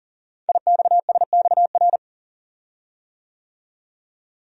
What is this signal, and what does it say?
Morse code, heard as a keyed tone, about 39 wpm, 690 Hz. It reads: IXSXR
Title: Code Groups - 5 Letter:Numbers - 40wpm